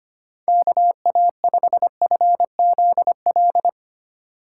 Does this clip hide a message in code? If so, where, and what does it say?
Yes: Morse code from 0.5 to 3.7 s, heard as a keyed tone, about 31 wpm, 700 Hz. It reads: KA5FZL